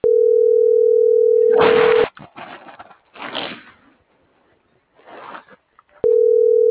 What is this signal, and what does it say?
Ringback tone (North America), 440 + 480 Hz.